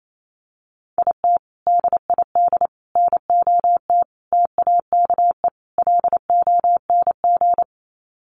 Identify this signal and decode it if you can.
Morse code, heard as a keyed tone, about 28 wpm, 700 Hz. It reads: IT DID NOT TAKE LONG
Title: Sets of 5 Words Encore2 28wpm